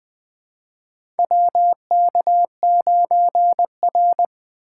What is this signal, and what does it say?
Morse code, heard as a keyed tone, about 20 wpm, 700 Hz. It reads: WK9R
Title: USA Call Signs - 4 Characters - 2x Character Spacing - 20wpm